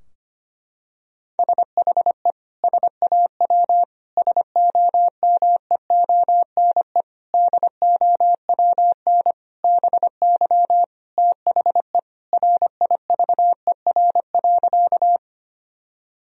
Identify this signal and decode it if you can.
Morse code, heard as a keyed tone, about 25 wpm, 700 Hz. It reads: SHE SAW SOMEONE DOWN BY THE RIVER.
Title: Sentences from Top 1000 Words - Review Part 4 - 25wpm